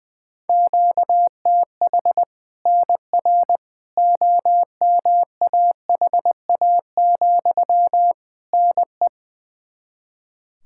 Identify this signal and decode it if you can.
Morse code, heard as a keyed tone, about 21 wpm, 700 Hz. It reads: QTH NR OMAHA, NE
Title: Practice QSO - Exchange 1 - Beginning - 20wpm